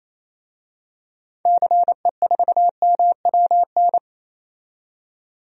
Morse code, heard as a keyed tone, about 28 wpm, 700 Hz. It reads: CE4MWN
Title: DXCC-100 Call Signs - 6 Characters - 28wpm